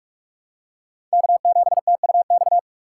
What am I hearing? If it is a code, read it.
Morse code, heard as a keyed tone, about 45 wpm, 690 Hz. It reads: K7TUX
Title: USA Call Signs - 5 Characters - 2x Character Spacing - 45wpm